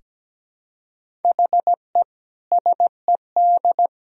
Morse code, heard as a keyed tone, about 17 wpm, 700 Hz. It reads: HE SED
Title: Sentences from Top 200 Words and Abbreviations 17wpm